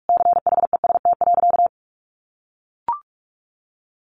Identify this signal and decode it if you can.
Morse code, heard as a keyed tone, about 44 wpm, 700 Hz. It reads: CHEST.
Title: A Sound Of Thunder 45wpm